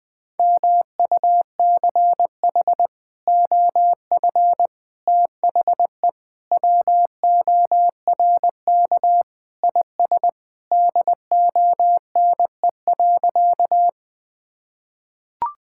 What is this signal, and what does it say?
Morse code, heard as a keyed tone, about 20 wpm, 700 Hz. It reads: MUCH OF THE WORK IS DONE.
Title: Sentences with 3-Word Phrases - 20wpm